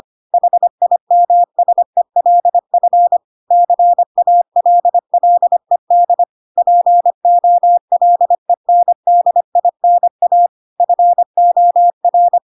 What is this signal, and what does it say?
Morse code, heard as a keyed tone, about 25 wpm, 700 Hz. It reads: HIMSELF CALLED POLENDINA FOR
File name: Pinocchio / Pinocchio ch2 25 25 ik1hge